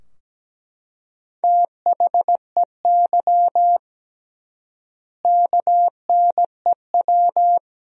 Morse code, heard as a keyed tone, about 17 wpm, 700 Hz. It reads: THEY KNEW
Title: 3x Word-Spacing - Sentences from Top 400 Words - 17wpm